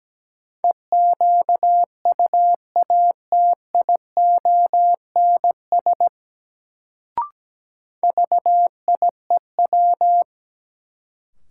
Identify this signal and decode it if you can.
Morse code, heard as a keyed tone, about 17 wpm, 700 Hz. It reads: EQUATIONS VIEW
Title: Sets of 5 Words - New Word Drill - 17wpm